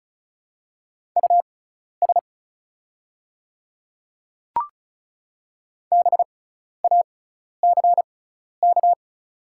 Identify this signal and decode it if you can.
Morse code, heard as a keyed tone, about 34 wpm, spaced as for 12 wpm, 700 Hz. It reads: US BACK
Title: 6x Character Spacing - Top 200 Words - 35wpm